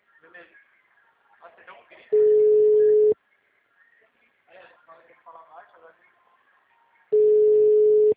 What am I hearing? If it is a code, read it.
Ringback tone (Europe, 425 Hz), with speech.
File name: out- -2012- - - .92995